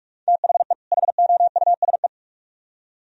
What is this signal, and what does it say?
Morse code, heard as a keyed tone, about 45 wpm, 700 Hz. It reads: THE HOUSE